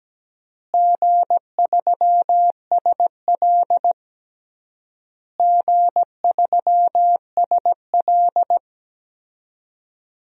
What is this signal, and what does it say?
Morse code, heard as a keyed tone, about 17 wpm, 700 Hz. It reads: G3SL G3SL
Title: DXCC-100 Call Signs - 4 Characters - Repeated 3x - 17wpm